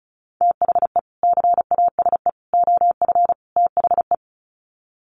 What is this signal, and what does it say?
Morse code, heard as a keyed tone, about 34 wpm, 700 Hz. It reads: THE CASE OF THE
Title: Sets of 4 Words 35wpm